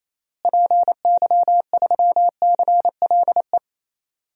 Morse code, heard as a keyed tone, about 28 wpm, 700 Hz. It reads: PY3CLE
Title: DXCC-100 Call Signs - 6 Characters - Repeated 2x - 28wpm